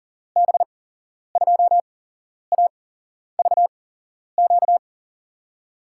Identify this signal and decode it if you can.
Morse code, heard as a keyed tone, about 40 wpm, spaced as for 11 wpm, 700 Hz. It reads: B2AVQ
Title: USA Call Signs - 6 Characters - 8x Character Spacing - 40wpm